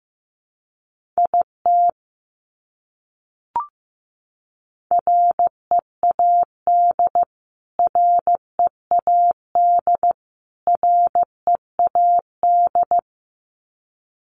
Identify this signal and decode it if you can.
Morse code, heard as a keyed tone, about 15 wpm, 700 Hz. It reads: IT READ READ READ
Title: Top 200 Words Repeated 15wpm